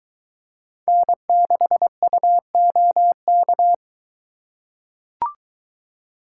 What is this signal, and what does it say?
Morse code, heard as a keyed tone, about 22 wpm, 700 Hz. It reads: N6UOK